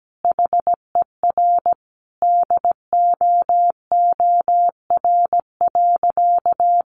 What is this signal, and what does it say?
Morse code, heard as a keyed tone, about 17 wpm, 700 Hz. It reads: HER DOOR.
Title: Sentences from Top 1400 Words 17wpm